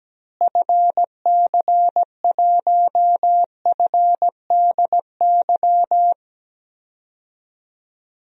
Morse code, heard as a keyed tone, about 17 wpm, 700 Hz. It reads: FC1FDY
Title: DXCC-100 Call Signs - 6 Characters - Repeated 3x - 17wpm